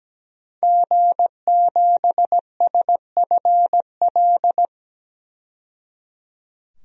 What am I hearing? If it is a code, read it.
Morse code, heard as a keyed tone, about 17 wpm, 700 Hz. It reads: G7SFL
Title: DXCC-100 Call Signs - 5 Characters - 17wpm